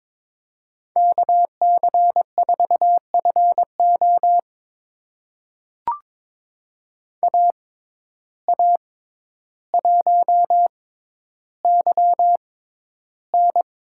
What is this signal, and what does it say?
Morse code, heard as a keyed tone, about 21 wpm, spaced as for 11 wpm, 700 Hz. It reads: KC4FO AA1YN
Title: USA Call Signs - 6x Character Spacing - 22wpm